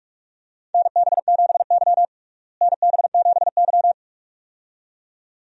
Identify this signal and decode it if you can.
Morse code, heard as a keyed tone, about 42 wpm, 700 Hz. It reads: NB7Y NB7Y
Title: USA Call Signs - 4 Characters - Repeated 2x - 45wpm